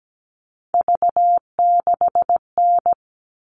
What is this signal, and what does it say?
Morse code, heard as a keyed tone, about 17 wpm, 700 Hz. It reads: V6N
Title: All Cut Numbers - 3-Digit 17wpm